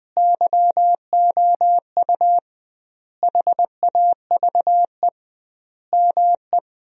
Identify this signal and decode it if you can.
Morse code, heard as a keyed tone, about 20 wpm, 700 Hz. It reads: YOU HAVE ME
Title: 2x Word-Spacing - Sentences from Top 200 Words - 20wpm